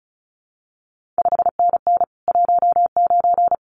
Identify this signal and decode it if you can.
Morse code, heard as a keyed tone, about 36 wpm, 700 Hz. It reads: 5NN 19
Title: CQ WW DX Contest Exchange Encore - 35wpm